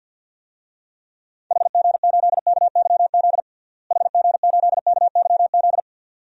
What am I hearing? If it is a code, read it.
Morse code, heard as a keyed tone, about 53 wpm, 700 Hz. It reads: HG8KYZ HG8KYZ
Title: DXCC-100 Call Signs - 6 Characters - Repeated 2x - 50wpm